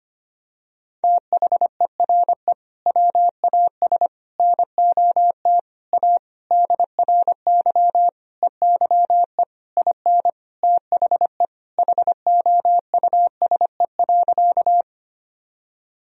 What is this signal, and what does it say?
Morse code, heard as a keyed tone, about 25 wpm, 700 Hz. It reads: THERE WAS NOT A DRY EYE IN THE HOUSE.
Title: Sentences from Top 1000 Words - Review Part 3 - 25wpm